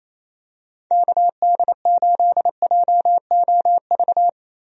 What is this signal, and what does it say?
Morse code, heard as a keyed tone, about 28 wpm, 700 Hz. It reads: KD8JOV
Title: USA Call Signs - 6 Characters - Repeated 3x - 28wpm